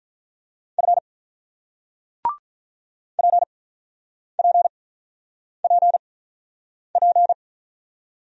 Morse code, heard as a keyed tone, about 48 wpm, 700 Hz. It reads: F P P P P